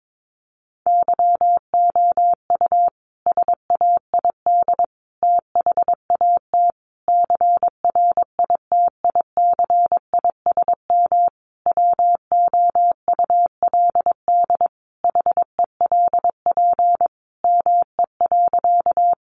Morse code, heard as a keyed tone, about 22 wpm, 700 Hz. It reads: YOU SAID THAT CRITICISM WOULD HELP ME.
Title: Sentences from Top 1300 Words 22wpm